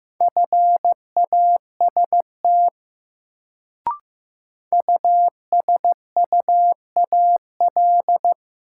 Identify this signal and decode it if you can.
Morse code, heard as a keyed tone, about 15 wpm, 700 Hz. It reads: FAST USUAL
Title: Sentences from Top 400 Words - New Word Drill - 15wpm